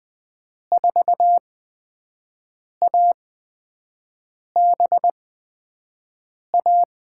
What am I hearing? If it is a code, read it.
Morse code, heard as a keyed tone, about 20 wpm, spaced as for 5 wpm, 700 Hz. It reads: 4ABA
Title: USA Call Signs - 8x Character Spacing - 20wpm